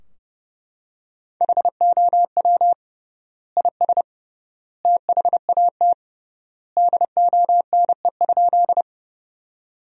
Morse code, heard as a keyed tone, about 30 wpm, 700 Hz. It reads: HOW IS THAT DONE?
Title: 3x Word-Spacing - Sentences from Top 500 Words - 30wpm